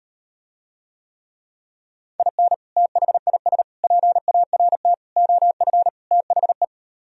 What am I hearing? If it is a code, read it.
Morse code, heard as a keyed tone, about 39 wpm, 700 Hz. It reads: IN THIS PART OF THE